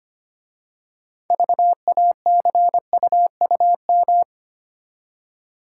Morse code, heard as a keyed tone, about 25 wpm, 700 Hz. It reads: VACUUM
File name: practice / 25wpm / Easy Words with Hints - 25wpm